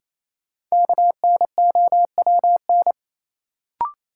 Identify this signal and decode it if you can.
Morse code, heard as a keyed tone, about 28 wpm, 700 Hz. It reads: KNOWN
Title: Sentences from Top 600 Words - New Word Drill - 28wpm